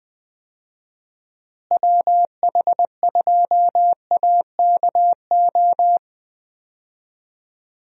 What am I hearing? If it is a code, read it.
Morse code, heard as a keyed tone, about 20 wpm, 700 Hz. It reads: WH2AKO
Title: USA Call Signs - 6 Characters - Repeated 2x - 20wpm